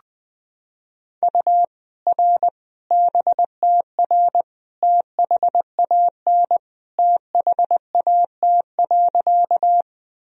Morse code, heard as a keyed tone, about 20 wpm, 700 Hz. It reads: U R BTR THAN THAT.
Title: Sentences from Top 400 Words and Abbreviations 20wpm